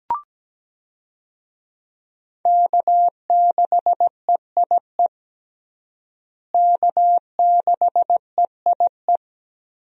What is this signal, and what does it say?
Morse code, heard as a keyed tone, about 17 wpm, 700 Hz. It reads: K6EIE K6EIE
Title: USA Call Signs - 5 Characters - Repeated 3x - 17wpm